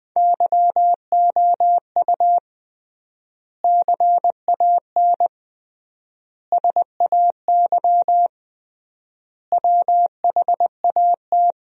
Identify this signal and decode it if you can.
Morse code, heard as a keyed tone, about 21 wpm, 700 Hz. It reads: YOU CAN SAY WHAT
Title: 3x Word-Spacing - Sentences from Top 200 Words - 20wpm